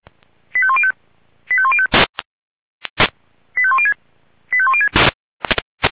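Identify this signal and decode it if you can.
Five-tone selective call (ZVEI), heard as quick runs of five tones, tones of about 70 ms. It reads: 85105 85105 85106 85106
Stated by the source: ZVEI five-tone selective call received on SDR.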